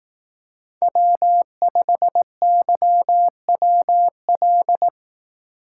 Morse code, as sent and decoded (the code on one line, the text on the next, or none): .-- ..... -.-- .-- .-..
W5YWL